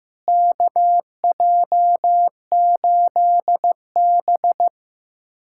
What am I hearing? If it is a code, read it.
Morse code, heard as a keyed tone, about 15 wpm, 700 Hz. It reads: KJ8B
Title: USA Callsign DE Callsign - 4 Characters - 15wpm